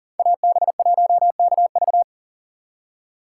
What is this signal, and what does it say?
Morse code, heard as a keyed tone, about 52 wpm, 700 Hz. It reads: AB1KV